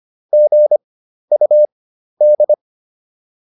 Morse code, heard as a keyed tone, about 25 wpm, spaced as for 12 wpm, 600 Hz. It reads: GUD